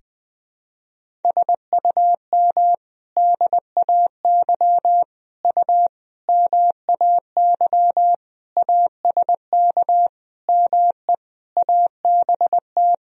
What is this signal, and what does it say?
Morse code, heard as a keyed tone, about 21 wpm, 700 Hz. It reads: SUM DAY U MAY ASK ME ABT
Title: Sentences from Top 200 Words and Abbreviations 20wpm